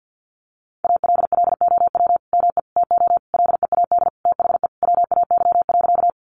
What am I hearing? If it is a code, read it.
Morse code, heard as a keyed tone, about 50 wpm, 700 Hz. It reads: ALLOW ME TO LEAD THE WAY.
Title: Sentences from Top 700 Words 50wpm